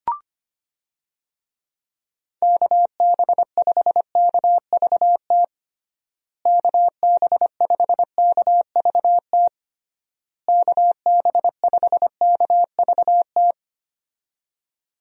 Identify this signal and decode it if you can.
Morse code, heard as a keyed tone, about 25 wpm, 700 Hz. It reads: KB5KVT KB5KVT KB5KVT
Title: USA Call Signs - 6 Characters - Repeated 3x - 25wpm